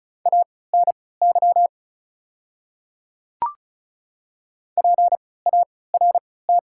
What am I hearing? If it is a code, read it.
Morse code, heard as a keyed tone, about 36 wpm, spaced as for 20 wpm, 700 Hz. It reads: ANY PART
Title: 3x Character Spacing - Top 200 Words - 35wpm